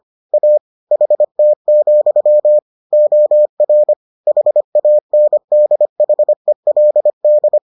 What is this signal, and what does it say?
Morse code, heard as a keyed tone, about 25 wpm, 600 Hz. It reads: A HT, OR HANDHELD